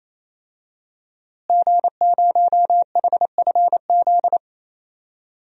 Morse code, heard as a keyed tone, about 28 wpm, 700 Hz. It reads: G0HFZ
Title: DXCC-100 Call Signs - 5 Characters - Repeated 3x - 28wpm